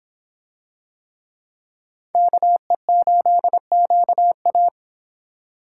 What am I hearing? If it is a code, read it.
Morse code, heard as a keyed tone, about 26 wpm, 700 Hz. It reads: KE8QA